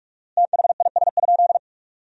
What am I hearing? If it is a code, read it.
Morse code, heard as a keyed tone, about 41 wpm, 700 Hz. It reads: THIS?
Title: 4x Word-Spacing - Sentences from Top 200 Words - 45wpm